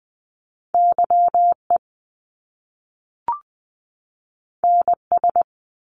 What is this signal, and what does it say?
Morse code, heard as a keyed tone, about 20 wpm, 700 Hz. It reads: YE NS
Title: Top Two Letter Combinations in Words 20wpm